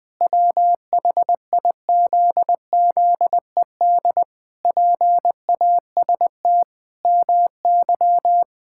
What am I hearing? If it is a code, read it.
Morse code, heard as a keyed tone, about 21 wpm, 700 Hz. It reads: WHIZZED PAST MY
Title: Pangram Challenge v2 - 20wpm